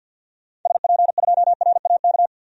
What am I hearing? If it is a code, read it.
Morse code, heard as a keyed tone, about 49 wpm, 700 Hz. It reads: SP2RAK
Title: DXCC-100 Call Signs - 6 Characters - 50wpm